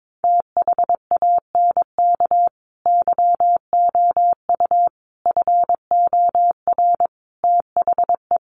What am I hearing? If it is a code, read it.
Morse code, heard as a keyed tone, about 22 wpm, 700 Hz. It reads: THANK YOU FOR THE
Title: Sentences from Top 600 Words 22wpm